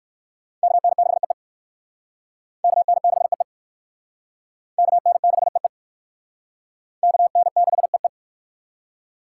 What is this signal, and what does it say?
Morse code, heard as a keyed tone, about 58 wpm, 700 Hz. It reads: KN6EE KN6EE KN6EE KN6EE